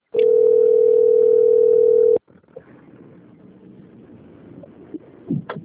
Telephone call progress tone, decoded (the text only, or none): ringback tone